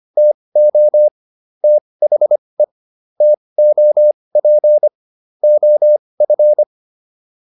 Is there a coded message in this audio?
Yes: Morse code from 0.2 to 6.6 s, heard as a keyed tone, about 25 wpm, spaced as for 20 wpm, 600 Hz. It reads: TO THE TOP OF